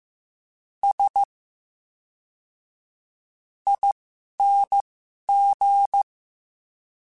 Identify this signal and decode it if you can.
Morse code, heard as a keyed tone, about 15 wpm, spaced as for 11 wpm, 800 Hz. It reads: S ING
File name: Special INGs / Special INGs-mp3s / VerySlowToFaster Ing @10